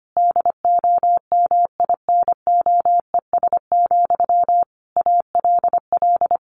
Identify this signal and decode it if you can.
Morse code, heard as a keyed tone, about 25 wpm, 700 Hz. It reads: DOMINOES, ALL
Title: A Sound Of Thunder 25wpm